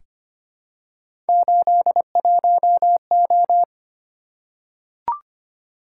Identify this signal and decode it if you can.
Morse code, heard as a keyed tone, about 25 wpm, 700 Hz. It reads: 81O
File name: Three Letter-Number Combinations - 25wpm